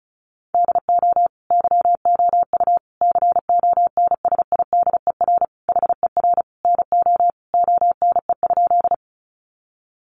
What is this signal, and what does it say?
Morse code, heard as a keyed tone, about 34 wpm, 700 Hz. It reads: DO YOU CONSIDER HER NO ONE?
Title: Sentences from Top 600 Words 35wpm